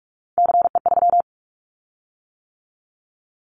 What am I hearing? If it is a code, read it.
Morse code, heard as a keyed tone, about 44 wpm, 700 Hz. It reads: CE3
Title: Callsign Prefixes for Top 100 DXCC - 45wpm